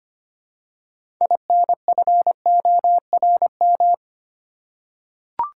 Morse code, heard as a keyed tone, about 25 wpm, 700 Hz. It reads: INFORM
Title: Six Letter Words - 25wpm